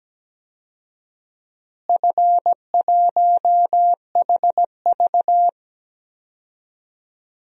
Morse code, heard as a keyed tone, about 17 wpm, 700 Hz. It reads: F1HV
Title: DXCC-100 Call Signs - 4 Characters - Repeated 3x - 17wpm